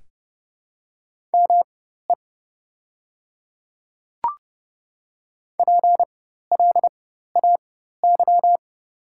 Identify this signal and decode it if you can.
Morse code, heard as a keyed tone, about 29 wpm, spaced as for 14 wpm, 700 Hz. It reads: ME PLAY
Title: 4x Character Spacing - Top 200 Words - 30wpm